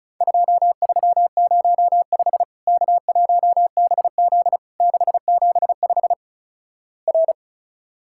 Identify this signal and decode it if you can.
Morse code, heard as a keyed tone, about 34 wpm, 700 Hz. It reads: 2305 K1BZ 675 R
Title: Worked All Europe DX Contest - QTC Exchange - 35wpm